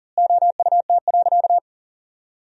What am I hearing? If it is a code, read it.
Morse code, heard as a keyed tone, about 37 wpm, 700 Hz. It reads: OUT.
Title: 4x Word-Spacing - Sentences from Top 300 Words - 40wpm